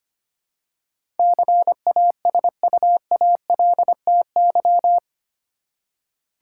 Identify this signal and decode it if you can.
Morse code, heard as a keyed tone, about 25 wpm, 700 Hz. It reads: CASUALTY
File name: Eight Letter Words - 25wpm